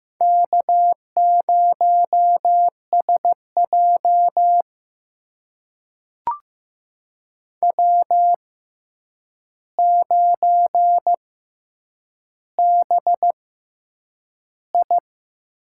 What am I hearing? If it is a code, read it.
Morse code, heard as a keyed tone, about 15 wpm, spaced as for 8 wpm, 700 Hz. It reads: K0SJ W9BI
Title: USA Call Signs - 4 Characters - 6x Character Spacing - 15wpm